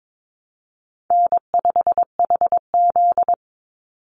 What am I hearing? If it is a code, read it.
Morse code, heard as a keyed tone, about 22 wpm, 700 Hz. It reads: N5HZ
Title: Easy Call Signs - 22wpm